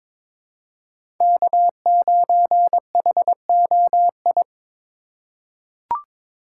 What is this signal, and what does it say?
Morse code, heard as a keyed tone, about 22 wpm, 700 Hz. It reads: K9HOI
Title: USA Call Signs - 5 Characters - 3x Character Spacing - 22wpm